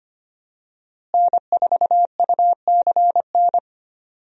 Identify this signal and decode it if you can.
Morse code, heard as a keyed tone, about 25 wpm, 700 Hz. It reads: N4UCN
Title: USA Call Signs - 5 Characters - Repeated 3x - 25wpm